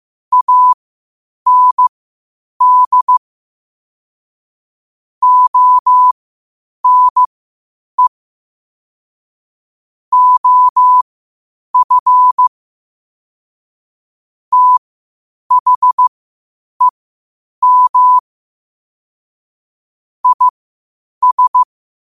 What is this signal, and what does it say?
Morse code, heard as a keyed tone, about 15 wpm, spaced as for 8 wpm, 1000 Hz. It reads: AND ONE OF THEM IS
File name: qotdfiles / 8wpm